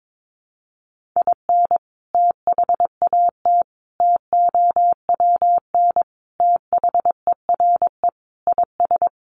Morse code, heard as a keyed tone, about 22 wpm, 700 Hz. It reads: IN THAT TOWN THERE IS
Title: Sentences from Top 500 Words 22wpm